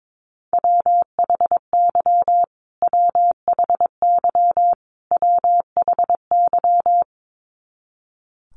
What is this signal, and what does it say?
Morse code, heard as a keyed tone, about 22 wpm, 700 Hz. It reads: WHY WHY WHY
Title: Top 500 Words Repeated 22wpm